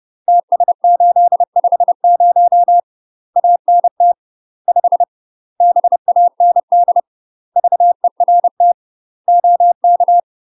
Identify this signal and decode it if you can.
Morse code, heard as a keyed tone, about 30 wpm, 700 Hz. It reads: TS850 ANT 5 BAND VERT OK
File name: UK MASTER QSOs TRAINING / 30WPM / UK MASTER QSOs 0030